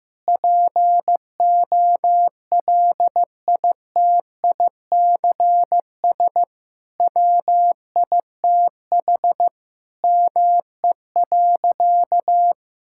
Morse code, heard as a keyed tone, about 15 wpm, 700 Hz. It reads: POLITICS WITH ME.
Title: Sentences from Top 1000 Words - Review Part 6 - 15wpm